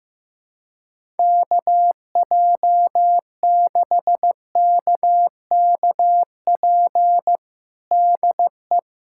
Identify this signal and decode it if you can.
Morse code, heard as a keyed tone, about 16 wpm, 700 Hz. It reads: KJ6KKP DE
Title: USA Callsign DE Callsign - 6 Characters - 15wpm